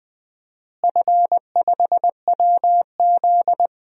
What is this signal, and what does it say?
Morse code, heard as a keyed tone, about 20 wpm, 700 Hz. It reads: F5WZ
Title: DXCC-100 Call Signs - 4 Characters - Repeated 3x - 20wpm